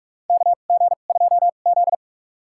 Morse code, heard as a keyed tone, about 41 wpm, spaced as for 31 wpm, 700 Hz. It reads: KG2Z
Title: USA Call Signs - 2x Character Spacing - 45wpm